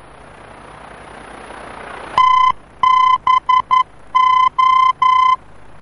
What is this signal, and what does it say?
Morse code, heard as a keyed tone, about 11 wpm, 1020 Hz. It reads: TBO
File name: france / mx / TBO-F-VOR